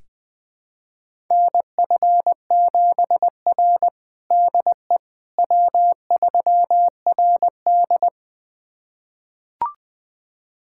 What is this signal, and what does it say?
Morse code, heard as a keyed tone, about 20 wpm, 700 Hz. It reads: NF7R DE W3RD
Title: USA Callsign DE Callsign - 4 Characters - 20wpm